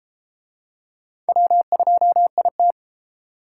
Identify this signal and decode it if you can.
Morse code, heard as a keyed tone, about 33 wpm, 700 Hz. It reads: W2IT